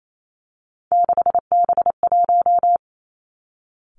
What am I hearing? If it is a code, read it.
Morse code, heard as a keyed tone, about 28 wpm, 700 Hz. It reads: 6B1
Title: Three Challenging Letter-Number Combinations 28wpm